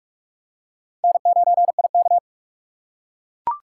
Morse code, heard as a keyed tone, about 44 wpm, 700 Hz. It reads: N9IK